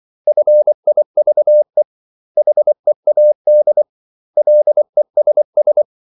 Morse code, heard as a keyed tone, about 24 wpm, 600 Hz. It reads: FIVE HEAD LESS
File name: CW / Practice / 500MostCommonEnglishWords24